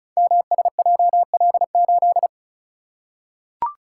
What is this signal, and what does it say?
Morse code, heard as a keyed tone, about 36 wpm, 700 Hz. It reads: MSJL8
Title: Code Groups - 5 Letter:Numbers - 35wpm